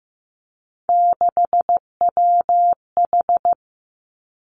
Morse code, heard as a keyed tone, about 15 wpm, 700 Hz. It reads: 6WH
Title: Three Challenging Letter-Number Combinations 15wpm